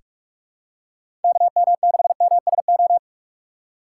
Morse code, heard as a keyed tone, about 46 wpm, 700 Hz. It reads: KM6MSO
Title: USA Call Signs - 6 Characters - 3x Character Spacing - 45wpm